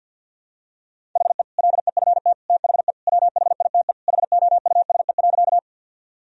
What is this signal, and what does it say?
Morse code, heard as a keyed tone, about 52 wpm, 700 Hz. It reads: HE LEFT THE WHITE HOUSE.
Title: Sentences from Top 1000 Words - Review Part 1 - 50wpm